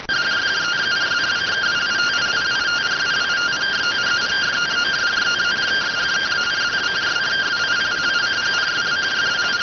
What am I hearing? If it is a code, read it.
Two-tone teleprinter signal, about 170 Hz shift.